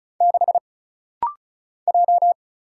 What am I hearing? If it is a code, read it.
Morse code, heard as a keyed tone, about 34 wpm, 700 Hz. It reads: 6 J
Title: Single Letter-Number Alt - Rapid-Fire 35wpm